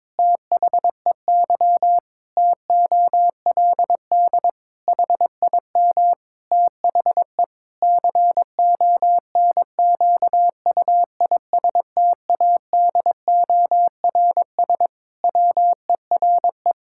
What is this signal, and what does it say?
Morse code, heard as a keyed tone, about 22 wpm, 700 Hz. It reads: THEY TOLD HIM THE CONQUISTADORS WERE
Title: Pangram Challenge v2 - 22wpm